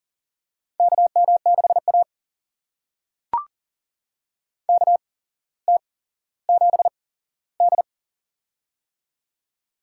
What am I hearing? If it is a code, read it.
Morse code, heard as a keyed tone, about 40 wpm, spaced as for 17 wpm, 700 Hz. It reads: KM6A KT7D
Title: USA Call Signs - 4 Characters - 8x Character Spacing - 40wpm